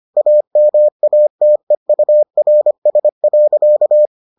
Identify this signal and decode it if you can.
Morse code, heard as a keyed tone, about 25 wpm, 600 Hz. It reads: AMATEURS.